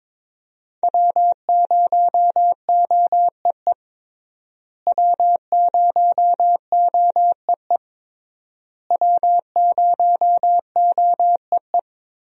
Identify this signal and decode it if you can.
Morse code, heard as a keyed tone, about 22 wpm, 700 Hz. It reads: W0OEE W0OEE W0OEE
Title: USA Call Signs - 5 Characters - Repeated 3x - 22wpm